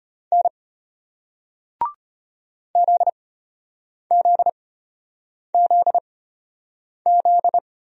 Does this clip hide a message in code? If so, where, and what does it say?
Yes: Morse code from 0.3 to 7.6 s, heard as a keyed tone, about 34 wpm, spaced as for 8 wpm, 700 Hz. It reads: N ZZZZ